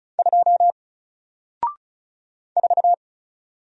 Morse code, heard as a keyed tone, about 35 wpm, 700 Hz. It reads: 2 4
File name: Single-Letter-Number-35wpm